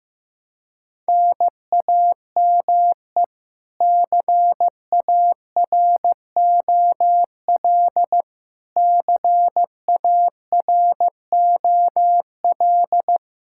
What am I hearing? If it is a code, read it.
Morse code, heard as a keyed tone, about 15 wpm, 700 Hz. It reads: NAME CAROL CAROL
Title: Practice QSO - Exchange 1 - Beginning - 15wpm